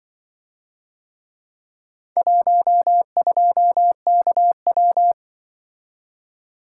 Morse code, heard as a keyed tone, about 25 wpm, 700 Hz. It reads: 12KW
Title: Speed-Racing - Four Letter-Number Combinations - 20wpm